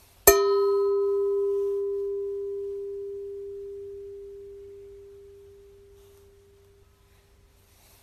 Dial tone (Japan), 400 Hz.